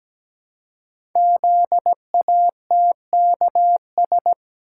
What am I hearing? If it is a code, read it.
Morse code, heard as a keyed tone, about 17 wpm, 700 Hz. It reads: ZATKS
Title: Code Groups - 5 Letters - 17wpm